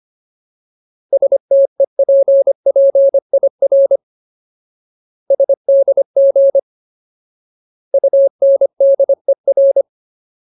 Morse code, heard as a keyed tone, about 25 wpm, 550 Hz. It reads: STEPPIR SDG UNDER
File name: Ultimate Word List / 25WPM Ultimate